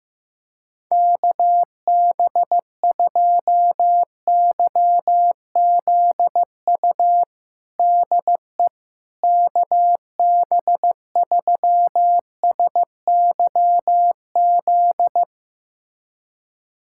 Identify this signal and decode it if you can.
Morse code, heard as a keyed tone, about 15 wpm, 700 Hz. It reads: KB2YZU DE KB3SYZ
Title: USA Callsign DE Callsign 15wpm